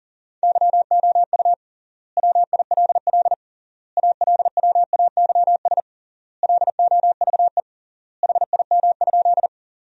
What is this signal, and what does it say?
Morse code, heard as a keyed tone, about 40 wpm, 700 Hz. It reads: YOU WILL ALWAYS LOVE HIM?
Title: 3x Word-Spacing - Sentences from Top 400 Words - 40wpm